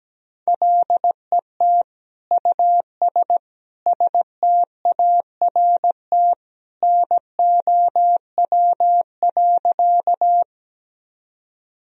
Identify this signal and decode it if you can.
Morse code, heard as a keyed tone, about 17 wpm, 700 Hz. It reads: LET US START NOW.
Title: Sentences from Top 1000 Words - Review Part 1 - 17wpm